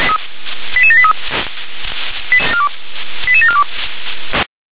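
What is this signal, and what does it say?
Five-tone selective call (ZVEI), heard as quick runs of five tones, tones of about 70 ms. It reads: 88783 88742
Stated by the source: ZVEI five-tone call received on an SDR.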